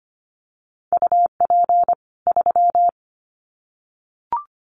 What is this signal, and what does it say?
Morse code, heard as a keyed tone, about 25 wpm, 700 Hz. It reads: UP 3
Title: Common QSO Phrases 25wpm